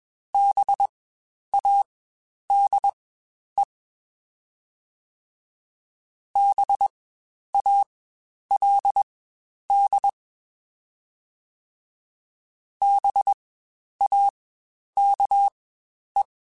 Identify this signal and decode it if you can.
Morse code, heard as a keyed tone, about 21 wpm, spaced as for 10 wpm, 800 Hz. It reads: BADE BALD BAKE
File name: Spaced Alphas / Spaced Alphas mp3s / Bs sp @20